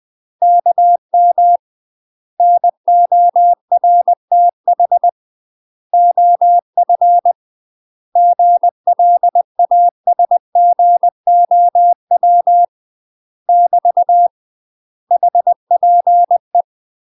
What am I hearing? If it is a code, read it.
Morse code, heard as a keyed tone, about 20 wpm, 700 Hz. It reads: KM NORTH OF GLASGOW <BT> HPE
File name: UK MASTER QSOs TRAINING / 20WPM / UK MASTER QSOs 0051